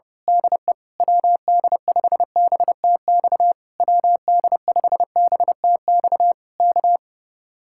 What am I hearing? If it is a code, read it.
Morse code, heard as a keyed tone, about 25 wpm, 700 Hz. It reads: DE WD5BTX WD5BTX K
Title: Speed-Racing Common QSO Phrases Encore - 25wpm